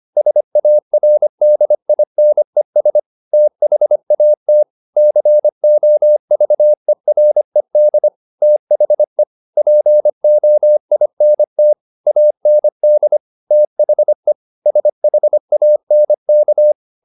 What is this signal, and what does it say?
Morse code, heard as a keyed tone, about 25 wpm, 600 Hz. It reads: SARDINES THAT COVERED THE POINT AND THE SHANK
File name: the old man and the sea part 1 25wpm